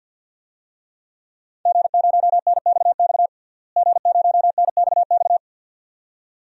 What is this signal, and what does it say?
Morse code, heard as a keyed tone, about 51 wpm, 700 Hz. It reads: G0NXX G0NXX
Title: DXCC-100 Call Signs - 5 Characters - Repeated 2x - 50wpm